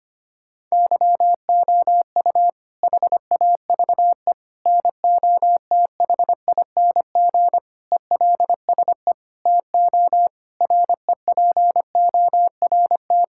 Morse code, heard as a keyed tone, about 25 wpm, 700 Hz. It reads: YOU HAVE NOTHING ELSE TO REPORT
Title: Sentences from Top 1000 Words - Review Part 6 - 25wpm